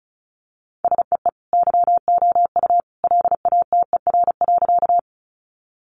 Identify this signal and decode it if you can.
Morse code, heard as a keyed tone, about 34 wpm, 700 Hz. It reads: SEE YOU LATER.
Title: Sentences from Top 700 Words 35wpm